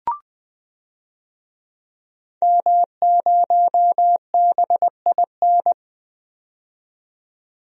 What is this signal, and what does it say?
Morse code, heard as a keyed tone, about 20 wpm, 700 Hz. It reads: M0BIN
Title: DXCC-100 Call Signs - 5 Characters - 20wpm